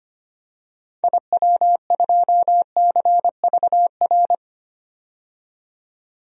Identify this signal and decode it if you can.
Morse code, heard as a keyed tone, about 25 wpm, 700 Hz. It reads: IW2CVR
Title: DXCC-100 Call Signs - 6 Characters - 25wpm